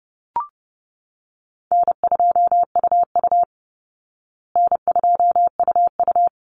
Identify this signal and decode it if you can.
Morse code, heard as a keyed tone, about 30 wpm, 700 Hz. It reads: N2UU N2UU
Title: Easy Call Signs Repeated 2x - 30wpm